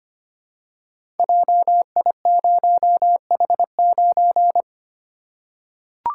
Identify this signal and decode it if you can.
Morse code, heard as a keyed tone, about 25 wpm, 700 Hz. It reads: JI0H9
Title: Code Groups - 5 Letter:Numbers - 25wpm